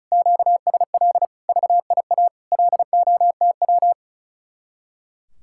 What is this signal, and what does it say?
Morse code, heard as a keyed tone, about 44 wpm, 700 Hz. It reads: QSL VIA LOTW